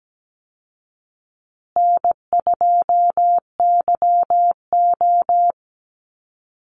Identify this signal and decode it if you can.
Morse code, heard as a keyed tone, about 17 wpm, 700 Hz. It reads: N2YO
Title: Easy Call Signs Repeated 2x - 17wpm